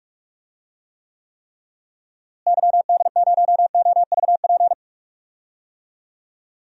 Morse code, heard as a keyed tone, about 43 wpm, 700 Hz. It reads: YD0OVP
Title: DXCC-100 Call Signs - 6 Characters - Repeated 3x - 45wpm